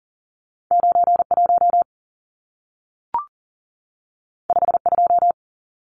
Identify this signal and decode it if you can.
Morse code, heard as a keyed tone, about 40 wpm, 700 Hz. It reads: 91 52
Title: Numbers - 2-Digit 40wpm